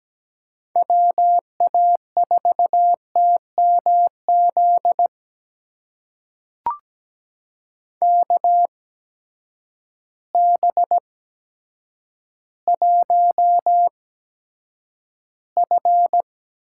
Morse code, heard as a keyed tone, about 17 wpm, spaced as for 8 wpm, 700 Hz. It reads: WA4TMZ KB1F
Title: USA Call Signs - 6 Characters - 8x Character Spacing - 17wpm